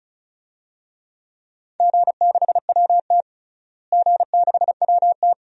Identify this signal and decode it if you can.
Morse code, heard as a keyed tone, about 36 wpm, 700 Hz. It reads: G6WT G6WT
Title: DXCC-100 Call Signs - 4 Characters - Repeated 2x - 35wpm